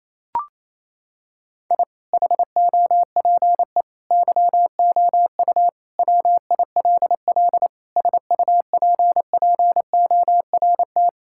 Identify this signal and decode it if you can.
Morse code, heard as a keyed tone, about 28 wpm, 700 Hz. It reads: I HOPE YOU WILL SUPPORT
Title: Sentences from Top 1000 Words - Review Part 5 - 28wpm